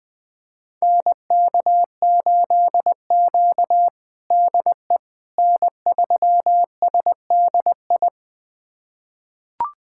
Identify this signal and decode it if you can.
Morse code, heard as a keyed tone, about 20 wpm, 700 Hz. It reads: NK8Q DE N3SDI
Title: USA Callsign DE Callsign 20wpm